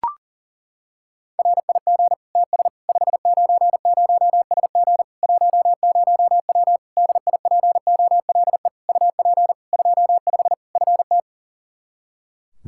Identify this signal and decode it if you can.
Morse code, heard as a keyed tone, about 40 wpm, 700 Hz. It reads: RIG TS 590SG 10W DIPOLE UP 25 FT
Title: K1USN SST QSO Practice 40wpm